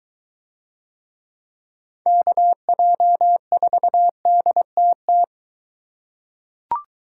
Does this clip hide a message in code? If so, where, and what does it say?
Yes: Morse code from 2.1 to 5.2 s, heard as a keyed tone, about 22 wpm, 700 Hz. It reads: KJ4DTT